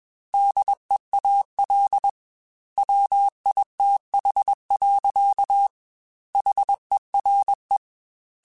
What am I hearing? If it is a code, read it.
Morse code, heard as a keyed tone, about 21 wpm, 800 Hz. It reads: DEAL WITH. HERE
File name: Technical Subjects / Technical Subjects-mp3s / AND Gate @20